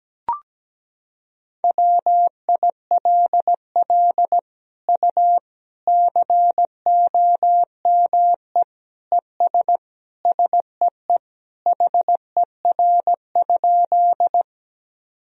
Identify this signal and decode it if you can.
Morse code, heard as a keyed tone, about 17 wpm, 700 Hz. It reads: WILL U COME ES SEE HER?
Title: Sentences from Top 100 Words and Abbreviations 17wpm